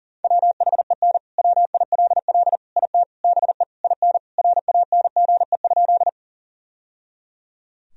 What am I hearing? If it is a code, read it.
Morse code, heard as a keyed tone, about 37 wpm, 700 Hz. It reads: WHEN WILL IT BE IN RANGE?
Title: Sentences from Top 1000 Words - Review Part 4 - 40wpm